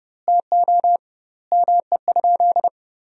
Morse code, heard as a keyed tone, about 30 wpm, 700 Hz. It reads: TO ME?
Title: 2x Word-Spacing - Sentences from Top 200 Words - 30wpm